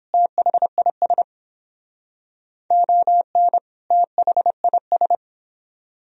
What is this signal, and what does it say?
Morse code, heard as a keyed tone, about 30 wpm, 700 Hz. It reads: THIS ON THIS